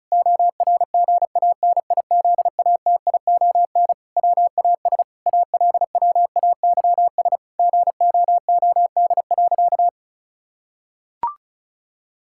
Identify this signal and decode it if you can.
Morse code, heard as a keyed tone, about 36 wpm, 700 Hz. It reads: ORGANIZATION WAS ALWAYS GOOD.
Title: Sentences from Top 1000 Words - Review Part 5 - 35wpm